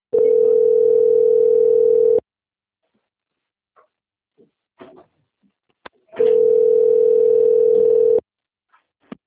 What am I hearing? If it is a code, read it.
Ringback tone (North America), 440 + 480 Hz.
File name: out- - - - - .11